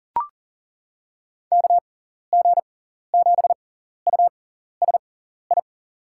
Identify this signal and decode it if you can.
Morse code, heard as a keyed tone, about 38 wpm, spaced as for 14 wpm, 700 Hz. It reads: KG7USI
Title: USA Call Signs - 6 Characters - 6x Character Spacing - 40wpm